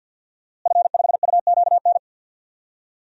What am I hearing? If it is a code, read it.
Morse code, heard as a keyed tone, about 52 wpm, 700 Hz. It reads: F5UQN
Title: DXCC-100 Call Signs - 5 Characters - Repeated 3x - 50wpm